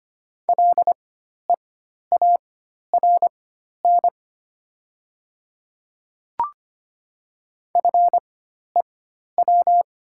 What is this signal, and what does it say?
Morse code, heard as a keyed tone, about 25 wpm, spaced as for 12 wpm, 700 Hz. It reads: LEARN FEW
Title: 4x Character Spacing - Top 300 Words - 25wpm